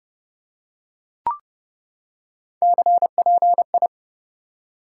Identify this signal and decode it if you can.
Morse code, heard as a keyed tone, about 30 wpm, 700 Hz. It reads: CPI
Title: Top 90 QSO Elements 30wpm